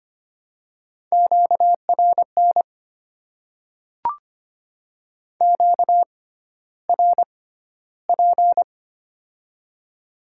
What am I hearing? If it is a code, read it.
Morse code, heard as a keyed tone, about 25 wpm, spaced as for 13 wpm, 690 Hz. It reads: QRN QRP
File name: Top 90 QSO Elements - 6x Character Spacing - 25wpm